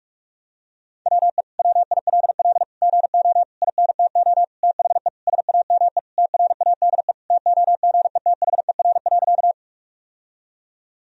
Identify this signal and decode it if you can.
Morse code, heard as a keyed tone, about 46 wpm, 700 Hz. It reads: WE WILL GO INTO THE SAME TRADE TOGETHER.
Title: Sentences from Top 1000 Words - Review Part 4 - 45wpm